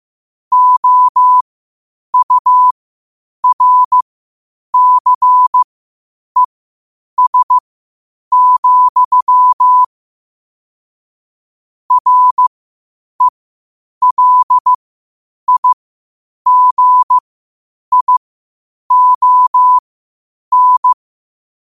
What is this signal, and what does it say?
Morse code, heard as a keyed tone, about 15 wpm, spaced as for 8 wpm, 1000 Hz. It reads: OURCES, RELIGION
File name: qotdfiles / 8wpm